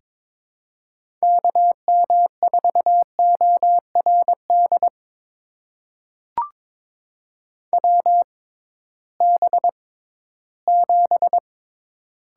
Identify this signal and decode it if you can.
Morse code, heard as a keyed tone, about 22 wpm, spaced as for 14 wpm, 700 Hz. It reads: KM4ORD WB7
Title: USA Call Signs - 6 Characters - 6x Character Spacing - 22wpm